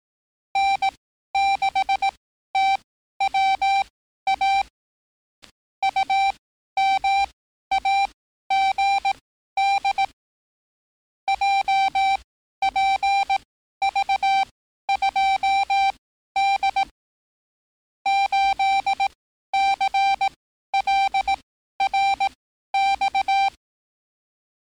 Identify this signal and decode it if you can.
Morse code, heard as a keyed tone, about 18 wpm, spaced as for 12 wpm, 780 Hz. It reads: N6TWA UMAGD JPV2D 8CLRX
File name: training / NAVPERS 11049to11060 / NAVPERS 11054 RA RB